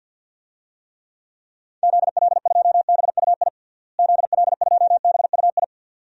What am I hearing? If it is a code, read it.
Morse code, heard as a keyed tone, about 53 wpm, 700 Hz. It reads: ZL2BUI ZL2BUI
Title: DXCC-100 Call Signs - 6 Characters - Repeated 2x - 50wpm